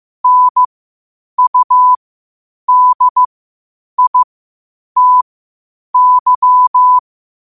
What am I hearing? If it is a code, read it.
Morse code, heard as a keyed tone, about 15 wpm, spaced as for 8 wpm, 1000 Hz. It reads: NUDITY
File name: qotdfiles / 8wpm